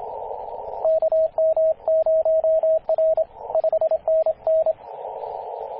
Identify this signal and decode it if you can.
Morse code, heard as a keyed tone, about 27 wpm, 640 Hz, with radio noise. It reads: KM0R 5NN
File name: t8cw-18-0737mhz-cw